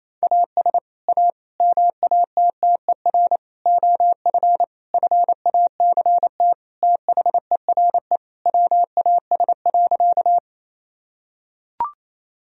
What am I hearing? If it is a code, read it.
Morse code, heard as a keyed tone, about 27 wpm, 700 Hz. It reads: AS A MATTER OF FACT THERE WAS.
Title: Sentences from Top 1000 Words - Review Part 3 - 28wpm